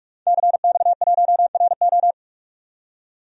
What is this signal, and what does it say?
Morse code, heard as a keyed tone, about 44 wpm, 690 Hz. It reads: CX1RO
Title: DXCC-100 Call Signs - 5 Characters - 45wpm